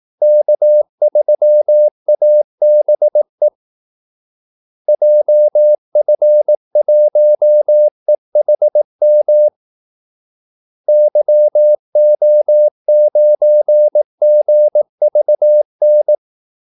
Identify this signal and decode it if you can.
Morse code, heard as a keyed tone, about 18 wpm, 600 Hz. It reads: K3ABE JF1EHM YO9GVN
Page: Kaldesignaler 18 wpm